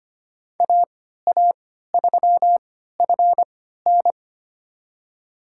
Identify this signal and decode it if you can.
Morse code, heard as a keyed tone, about 25 wpm, spaced as for 14 wpm, 700 Hz. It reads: AA3FN
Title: USA Call Signs - 5 Characters - 3x Character Spacing - 25wpm